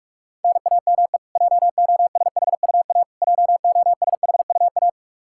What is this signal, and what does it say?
Morse code, heard as a keyed tone, about 46 wpm, 700 Hz. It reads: NAME JOSHUA JOSHUA
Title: Practice QSO - Exchange 1 - Beginning - 45wpm